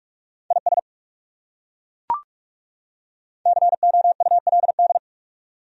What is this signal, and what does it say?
Morse code, heard as a keyed tone, about 40 wpm, 700 Hz. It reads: IS COULD